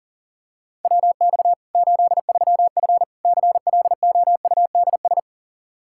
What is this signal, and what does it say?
Morse code, heard as a keyed tone, about 40 wpm, 700 Hz. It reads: WX 83F CLOUDS
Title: Practice QSO - Exchange 2 - Info - 40wpm